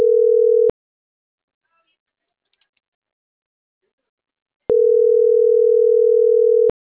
Ringback tone (North America), 440 + 480 Hz.